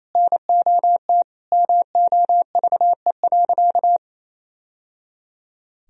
Morse code, heard as a keyed tone, about 28 wpm, 700 Hz. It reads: NOT MOVE.
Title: Sentences from Top 1000 Words - Review Part 1 - 28wpm